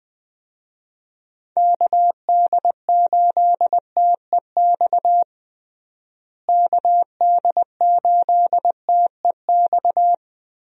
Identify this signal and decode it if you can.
Morse code, heard as a keyed tone, about 20 wpm, 700 Hz. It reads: KD8TEX KD8TEX
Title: USA Call Signs - 6 Characters - Repeated 3x - 20wpm